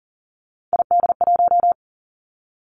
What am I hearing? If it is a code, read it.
Morse code, heard as a keyed tone, about 41 wpm, 700 Hz. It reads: ID1
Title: Three Challenging Letter-Number Combinations 40wpm